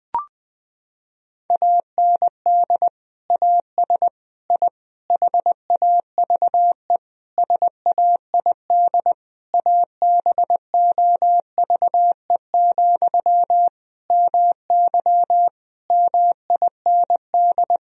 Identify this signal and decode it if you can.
Morse code, heard as a keyed tone, about 20 wpm, 700 Hz. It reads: AND AS I HAVE SAID ABOVE, MY MIND
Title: Sentences from Top 1000 Words - Review Part 2 - 20wpm